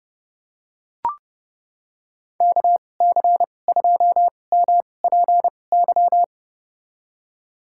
Morse code, heard as a keyed tone, about 30 wpm, spaced as for 22 wpm, 700 Hz. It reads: KC2MPY
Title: USA Call Signs - 2x Character Spacing - 30wpm